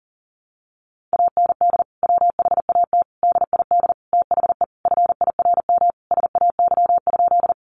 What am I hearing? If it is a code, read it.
Morse code, heard as a keyed tone, about 40 wpm, 700 Hz. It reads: AND WHAT DID THE FIRM SAY?
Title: Sentences from Top 1400 Words 40wpm